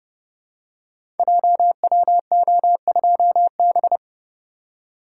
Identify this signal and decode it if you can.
Morse code, heard as a keyed tone, about 30 wpm, 700 Hz. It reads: JWO2B
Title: Code Groups - 5 Letter:Numbers - 30wpm